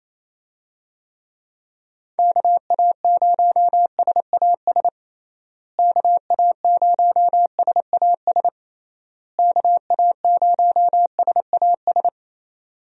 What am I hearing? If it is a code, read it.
Morse code, heard as a keyed tone, about 28 wpm, 700 Hz. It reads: KA0SAS KA0SAS KA0SAS
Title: USA Call Signs - 6 Characters - Repeated 3x - 28wpm